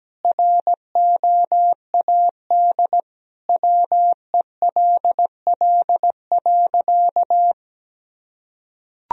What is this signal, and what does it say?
Morse code, heard as a keyed tone, about 17 wpm, 700 Hz. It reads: ROAD WELL.
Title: Sentences from Top 1000 Words - Review Part 2 - 17wpm